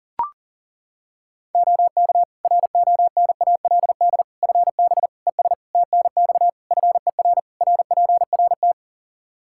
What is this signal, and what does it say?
Morse code, heard as a keyed tone, about 37 wpm, 700 Hz. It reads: OK RONALD FB ES TNX FER RPRT
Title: Practice QSO - Exchange 2 - Info - 40wpm